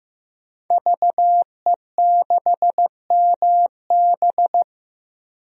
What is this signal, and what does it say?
Morse code, heard as a keyed tone, about 15 wpm, 700 Hz. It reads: VE6MB
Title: DXCC-100 Call Signs - 5 Characters - 15wpm